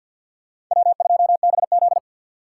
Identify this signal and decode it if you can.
Morse code, heard as a keyed tone, about 52 wpm, 700 Hz. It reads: W2BZ
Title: USA Call Signs - 4 Characters - 2x Character Spacing - 50wpm